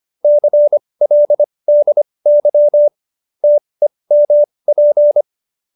Morse code, heard as a keyed tone, about 25 wpm, spaced as for 20 wpm, 590 Hz. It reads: CLDY TEMP
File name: qso208b 20